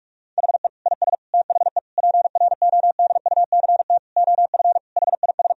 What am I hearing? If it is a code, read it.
Morse code, heard as a keyed tone, about 46 wpm, 700 Hz. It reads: HE IS THE PRODUCT OF HIS
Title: Sentences from Top 1000 Words - Review Part 2 - 45wpm